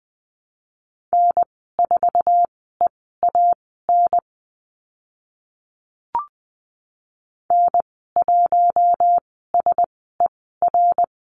Morse code, heard as a keyed tone, about 20 wpm, spaced as for 14 wpm, 700 Hz. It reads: N4EAN N1SER
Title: 062 Lesson 11g - Letter h - Call Signs 20wpm